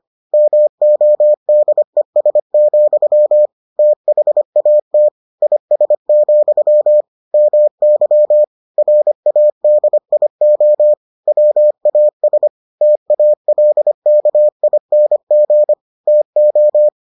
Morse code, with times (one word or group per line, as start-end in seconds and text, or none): 0.3-3.5 MODES,
3.8-5.1 THAT
5.4-7.0 IS,
7.3-8.4 MY
8.8-10.9 RADIO
11.3-12.5 WAS
12.8-15.7 TALKING
16.1-16.9 TO